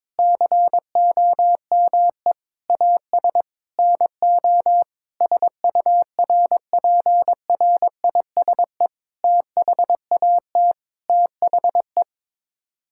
Morse code, heard as a keyed tone, about 22 wpm, 700 Hz. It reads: COME AS NO SURPRISE THAT THE
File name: practice / 22wpm / Sets of 8 Words - 22wpm